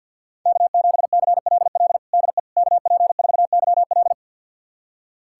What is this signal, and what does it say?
Morse code, heard as a keyed tone, about 47 wpm, 700 Hz. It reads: K7CLL DE KP4YL
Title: USA Callsign DE Callsign - 5 Characters - 50wpm